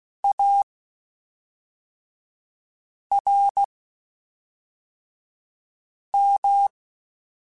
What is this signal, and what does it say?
Morse code, heard as a keyed tone, about 16 wpm, spaced as for 3 wpm, 800 Hz. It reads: ARM
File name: Quotes Spaced / Quotes Spaced-mp3s / ChildAsleep sp @5